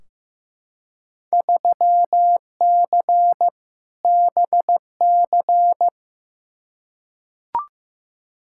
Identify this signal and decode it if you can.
Morse code, heard as a keyed tone, about 15 wpm, 700 Hz. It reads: 3C BC
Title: Field Day Exchanges Encore - 15wpm